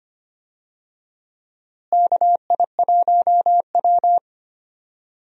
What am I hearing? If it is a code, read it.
Morse code, heard as a keyed tone, about 25 wpm, 700 Hz. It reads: KI1W
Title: USA Call Signs - 4 Characters - Repeated 3x - 25wpm